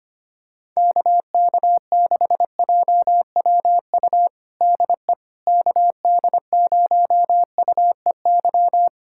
Morse code, heard as a keyed tone, about 25 wpm, 700 Hz. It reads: KK6JWU DE KD0UEY
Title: USA Callsign DE Callsign 25wpm